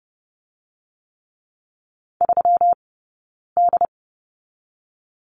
Morse code, heard as a keyed tone, about 30 wpm, 700 Hz. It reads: 3 D
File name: Lesson 20d - Number 3 - Two Characters 30wpm